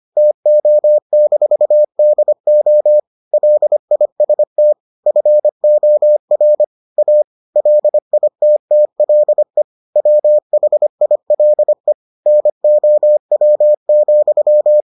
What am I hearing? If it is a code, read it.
Morse code, heard as a keyed tone, about 25 wpm, 600 Hz. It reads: TO-DO LIST FOR A LITTLE WHILE NOW,